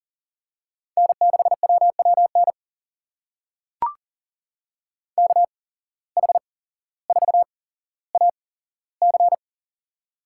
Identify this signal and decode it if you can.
Morse code, heard as a keyed tone, about 40 wpm, spaced as for 17 wpm, 700 Hz. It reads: N6WWN KH4AC
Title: USA Call Signs - 5 Characters - 8x Character Spacing - 40wpm